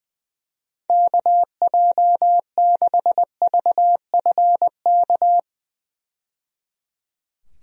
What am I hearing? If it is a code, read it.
Morse code, heard as a keyed tone, about 20 wpm, 700 Hz. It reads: KJ6VFK
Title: USA Call Signs - 6 Characters - 20wpm